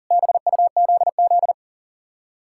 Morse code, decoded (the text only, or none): BUZZ